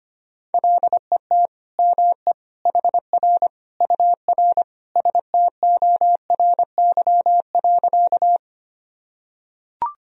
Morse code, heard as a keyed tone, about 25 wpm, 700 Hz. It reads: LET ME HR UR STORY.
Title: Sentences from Top 300 Words and Abbreviations 25wpm